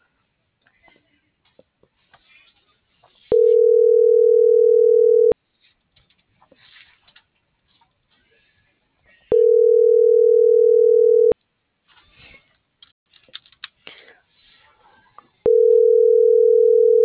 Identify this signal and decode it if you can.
Ringback tone (North America), 440 + 480 Hz.